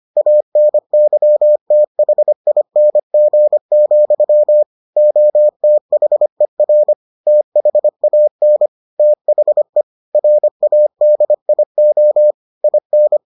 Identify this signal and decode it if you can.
Morse code, heard as a keyed tone, about 25 wpm, 600 Hz. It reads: ANYTHING, OTHER THAN THE RADIO IN